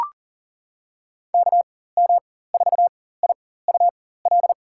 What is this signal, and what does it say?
Morse code, heard as a keyed tone, about 40 wpm, spaced as for 19 wpm, 700 Hz. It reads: KM4IUL
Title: USA Call Signs - 6 Characters - 4x Character Spacing - 40wpm